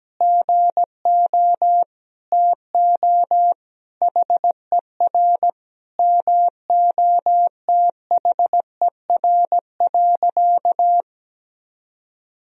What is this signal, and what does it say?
Morse code, heard as a keyed tone, about 17 wpm, 700 Hz. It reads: GO TO HER MOTHER.
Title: Sentences from Top 1000 Words - Review Part 4 - 17wpm